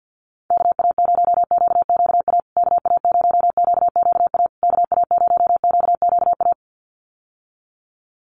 Morse code, heard as a keyed tone, about 48 wpm, 700 Hz. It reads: KA0QQA KA0QQA KA0QQA
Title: USA Call Signs Repeated 3x 50wpm